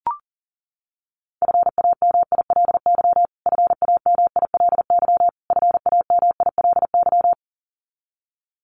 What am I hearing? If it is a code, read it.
Morse code, heard as a keyed tone, about 40 wpm, 700 Hz. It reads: FAMILY FAMILY FAMILY
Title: Top 500 Words Repeated 40wpm